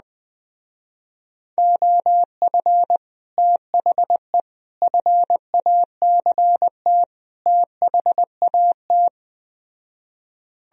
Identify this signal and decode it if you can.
Morse code, heard as a keyed tone, about 20 wpm, 700 Hz. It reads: OF THE FACT THAT
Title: Speed-Racing - Sets of 4 Words 17wpm